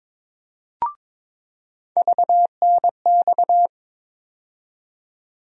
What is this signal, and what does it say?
Morse code, heard as a keyed tone, about 21 wpm, 700 Hz. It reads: VNX
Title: USA Call Sign Postfixes - 3 Letters 22wpm